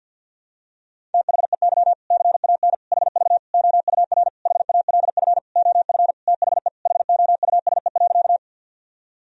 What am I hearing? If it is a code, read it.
Morse code, heard as a keyed tone, about 51 wpm, 700 Hz. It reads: THEY CAN HV OUR HALF OF THE HOUSE.
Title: Sentences from Top 400 Words and Abbreviations 50wpm